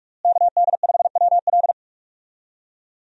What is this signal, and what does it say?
Morse code, heard as a keyed tone, about 43 wpm, 700 Hz. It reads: KD5WL
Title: USA Call Signs - 5 Characters - Repeated 2x - 45wpm